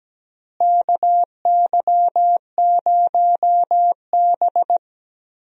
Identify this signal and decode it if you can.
Morse code, heard as a keyed tone, about 17 wpm, 700 Hz. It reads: KY0B
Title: USA Call Signs - 4 Characters - Repeated 3x - 17wpm